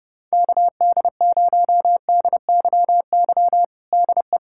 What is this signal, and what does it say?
Morse code, heard as a keyed tone, about 31 wpm, 700 Hz. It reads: KD0DYY DE
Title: USA Callsign DE Callsign - 6 Characters - 30wpm